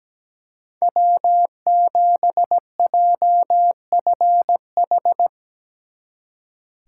Morse code, heard as a keyed tone, about 19 wpm, 700 Hz. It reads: W7JFH
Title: Speed Racing - USA 1x3 Call Signs 17wpm